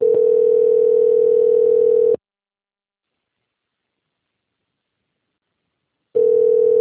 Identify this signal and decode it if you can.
Ringback tone (North America), 440 + 480 Hz.